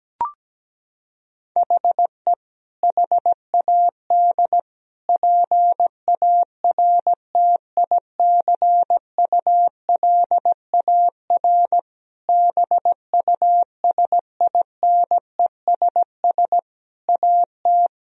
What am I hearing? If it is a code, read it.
Morse code, heard as a keyed tone, about 17 wpm, 700 Hz. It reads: HE HAD PARTICULAR BUSINESS AT
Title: Sentences from Top 1000 Words - Review Part 4 - 17wpm